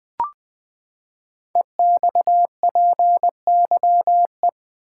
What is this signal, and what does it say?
Morse code, heard as a keyed tone, about 20 wpm, 700 Hz. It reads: EXPYE
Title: Code Groups - 5 Letters - 20wpm